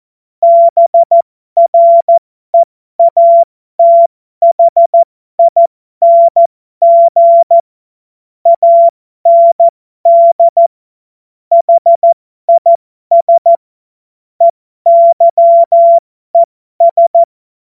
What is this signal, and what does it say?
Morse code, heard as a keyed tone, about 14 wpm, spaced as for 12 wpm, 680 Hz. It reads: BREATHING AND HIS EYES
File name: Morsekurs / Witze